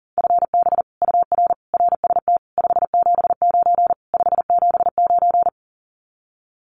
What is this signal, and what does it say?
Morse code, heard as a keyed tone, about 38 wpm, 700 Hz. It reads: FB UR RST 579 579
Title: Common QSO Phrases Encore 40wpm